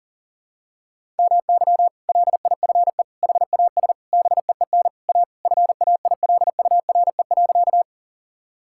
Morse code, heard as a keyed tone, about 37 wpm, 700 Hz. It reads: MY LIFE HAS BEEN A FAILURE.
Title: Sentences from Top 1000 Words - Review Part 6 - 40wpm